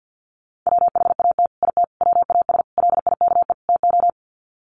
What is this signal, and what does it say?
Morse code, heard as a keyed tone, about 48 wpm, 700 Hz. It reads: WHAT IT WAS LIKE TO
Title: Sets of 5 Words Encore 50wpm